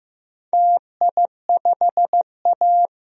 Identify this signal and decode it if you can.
Morse code, heard as a keyed tone, about 15 wpm, 700 Hz. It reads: TI5A
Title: DXCC-100 Call Signs - 4 Characters - Repeated 3x - 15wpm